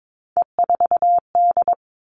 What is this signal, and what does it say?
Morse code, heard as a keyed tone, about 22 wpm, 700 Hz. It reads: E4D
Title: All Cut Numbers - 3-Digit 22wpm